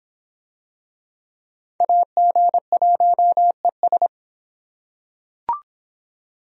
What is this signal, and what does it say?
Morse code, heard as a keyed tone, about 23 wpm, 700 Hz. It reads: AG1ES